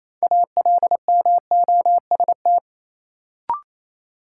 Morse code, heard as a keyed tone, about 28 wpm, 700 Hz. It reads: ALMOST
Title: Sentences from Top 700 Words - New Word Drill - 28wpm